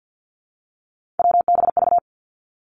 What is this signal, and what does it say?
Morse code, heard as a keyed tone, about 50 wpm, 700 Hz. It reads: WBV
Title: Three Challenging Letter-Number Combinations 50wpm